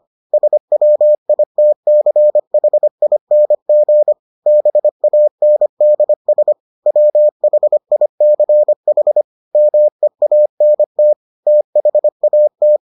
Morse code, heard as a keyed tone, about 25 wpm, 600 Hz. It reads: SWITCHING BANDS WHICH MEANT THAT